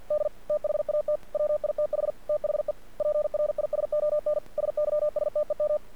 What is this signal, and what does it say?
Morse code, heard as a keyed tone, about 50 wpm, 600 Hz. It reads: D THAT WITH THE PRISON SYSTEM
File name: archived / CRSnet